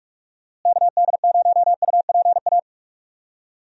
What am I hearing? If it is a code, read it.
Morse code, heard as a keyed tone, about 46 wpm, 700 Hz. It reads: KD0UPA
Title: USA Call Signs - 6 Characters - Repeated 2x - 45wpm